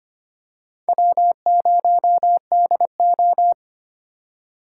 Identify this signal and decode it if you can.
Morse code, heard as a keyed tone, about 25 wpm, 700 Hz. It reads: W0DO
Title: USA Call Signs - 4 Characters - 2x Character Spacing - 25wpm